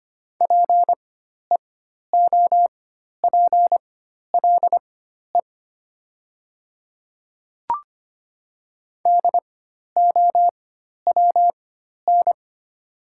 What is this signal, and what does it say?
Morse code, heard as a keyed tone, about 25 wpm, spaced as for 12 wpm, 700 Hz. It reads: PEOPLE DOWN
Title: 4x Character Spacing - Top 100 Words - 25wpm